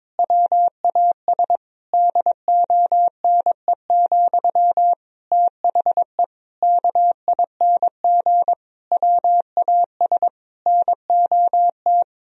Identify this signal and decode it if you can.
Morse code, heard as a keyed tone, about 22 wpm, 700 Hz. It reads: WAS DONE, THE KING WAS NOT
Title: Sentences from Top 1000 Words - Review Part 3 - 22wpm